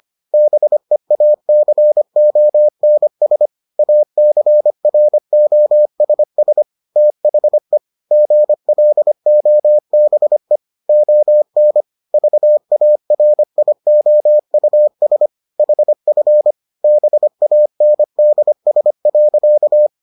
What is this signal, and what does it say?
Morse code, heard as a keyed tone, about 25 wpm, 600 Hz. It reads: BEACONS ACROSS THE GLOBE ON VARIOUS HF BANDS.